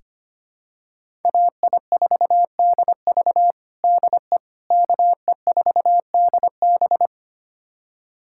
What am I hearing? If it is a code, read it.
Morse code, heard as a keyed tone, about 25 wpm, 700 Hz. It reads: AI4DV DE KE4DB
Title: USA Callsign DE Callsign - 5 Characters - 25wpm